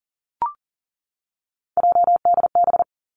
Morse code, heard as a keyed tone, about 41 wpm, 700 Hz. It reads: JDB
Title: Three Challenging Letter-Number Combinations 40wpm